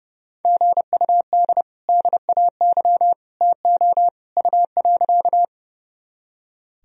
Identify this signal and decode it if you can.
Morse code, heard as a keyed tone, about 31 wpm, 700 Hz. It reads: GUD DAY TO U.
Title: Sentences from Top 200 Words and Abbreviations 30wpm